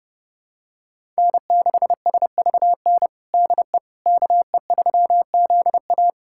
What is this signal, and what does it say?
Morse code, heard as a keyed tone, about 30 wpm, 700 Hz. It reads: N6SVN DE KE3ZA
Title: USA Callsign DE Callsign - 5 Characters - 30wpm